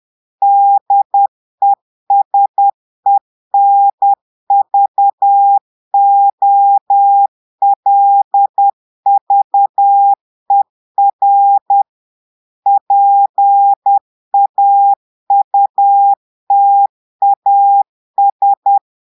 Morse code, heard as a keyed tone, about 10 wpm, 800 Hz. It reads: DESENVOLVER PAUTAS
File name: portugais texte en clair 2 vitesse 10 mots minutes